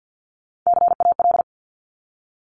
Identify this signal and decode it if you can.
Morse code, heard as a keyed tone, about 50 wpm, 700 Hz. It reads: CAL
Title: Top Three Letter Combinations in Words 50wpm